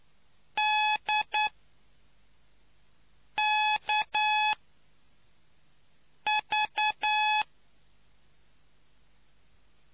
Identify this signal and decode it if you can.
Morse code, heard as a keyed tone, about 9 wpm, 850 Hz. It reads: D K V